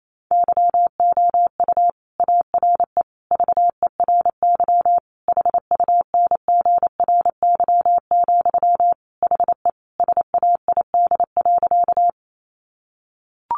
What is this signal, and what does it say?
Morse code, heard as a keyed tone, about 28 wpm, 700 Hz. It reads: YOU ARE VERY HUNGRY, HE SAID.
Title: Sentences from Top 1400 Words 28wpm